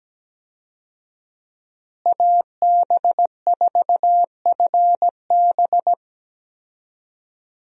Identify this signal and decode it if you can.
Morse code, heard as a keyed tone, about 17 wpm, 700 Hz. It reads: AB4FB
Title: USA Call Signs - 5 Characters - Repeated 3x - 17wpm